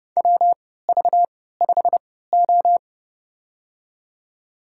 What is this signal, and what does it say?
Morse code, heard as a keyed tone, about 30 wpm, spaced as for 17 wpm, 700 Hz. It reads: WV5O
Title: USA Call Signs - 4 Characters - 3x Character Spacing - 30wpm